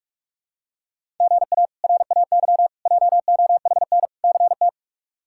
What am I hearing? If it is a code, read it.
Morse code, heard as a keyed tone, about 46 wpm, 700 Hz. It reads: GA RAY JOHN CT
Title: K1USN SST QSO Practice - 45wpm